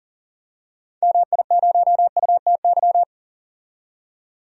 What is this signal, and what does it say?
Morse code, heard as a keyed tone, about 40 wpm, 700 Hz. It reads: MI0UTY
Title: DXCC-100 Call Signs - 6 Characters - 40wpm